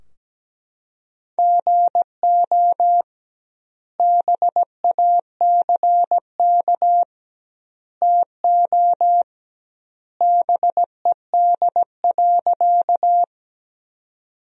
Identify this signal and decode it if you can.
Morse code, heard as a keyed tone, about 17 wpm, 700 Hz. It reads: GO BACK TO BED.
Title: 2x Word-Spacing - Sentences from Top 500 Words - 17wpm